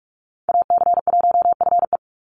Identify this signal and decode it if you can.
Morse code, heard as a keyed tone, about 44 wpm, 700 Hz. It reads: AC1FE
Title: Call Signs - 45wpm